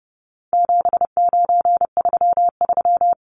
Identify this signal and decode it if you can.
Morse code, heard as a keyed tone, about 30 wpm, 700 Hz. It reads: 7933
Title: Numbers - 4-Digit 30wpm